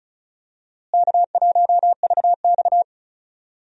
Morse code, heard as a keyed tone, about 36 wpm, 700 Hz. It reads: K1VX
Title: USA Call Signs - 4 Characters - Repeated 3x - 35wpm